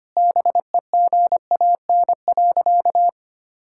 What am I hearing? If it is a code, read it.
Morse code, heard as a keyed tone, about 25 wpm, 700 Hz. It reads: BEGAN.
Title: Sentences from Top 1000 Words - Review Part 5 - 25wpm